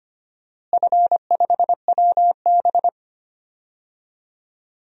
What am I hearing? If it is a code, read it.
Morse code, heard as a keyed tone, about 25 wpm, 700 Hz. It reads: F5WB
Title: DXCC-100 Call Signs - 4 Characters - Repeated 2x - 25wpm